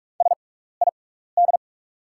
Morse code, heard as a keyed tone, about 44 wpm, spaced as for 15 wpm, 690 Hz. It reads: SID
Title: 6x Character Spacing - Top 500 Words - 45wpm